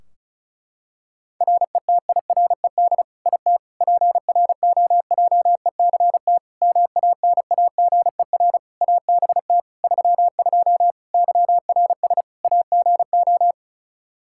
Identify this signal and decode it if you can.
Morse code, heard as a keyed tone, about 34 wpm, 700 Hz. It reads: RETIRED IT PROJECT MANAGER ABT 32 YRS AGO
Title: Practice QSO - Exchange 3 - More Info - 35wpm